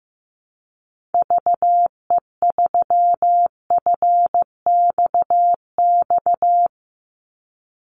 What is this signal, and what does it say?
Morse code, heard as a keyed tone, about 15 wpm, 700 Hz. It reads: VE3FXX
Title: Call Signs Repeated 3x - 15wpm